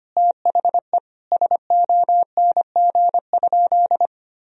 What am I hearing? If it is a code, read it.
Morse code, heard as a keyed tone, about 25 wpm, 700 Hz. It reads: THE SONG?
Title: Sentences from Top 1000 Words - Review Part 2 - 25wpm